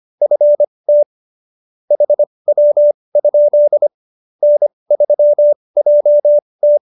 Morse code, heard as a keyed tone, about 25 wpm, spaced as for 20 wpm, 590 Hz. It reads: FT HW? N3JT
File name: QSO307 20